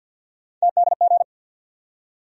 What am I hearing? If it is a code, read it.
Morse code, heard as a keyed tone, about 48 wpm, 700 Hz. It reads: TDG
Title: USA Call Sign Postfixes - 3 Letters 50wpm